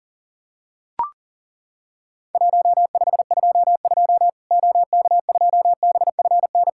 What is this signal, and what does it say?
Morse code, heard as a keyed tone, about 40 wpm, 700 Hz. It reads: 1522 OK2BFN
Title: Worked All Europe DX Contest - QTC Exchange - 40wpm